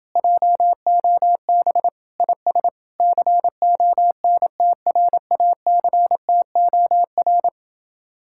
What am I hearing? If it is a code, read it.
Morse code, heard as a keyed tone, about 29 wpm, 700 Hz. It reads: JOB IS CONTRACTOR